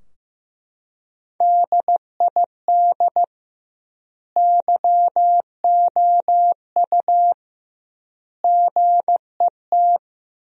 Morse code, heard as a keyed tone, about 16 wpm, 700 Hz. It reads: DID YOU GET
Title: 2x Word-Spacing - Sentences from Top 400 Words - 15wpm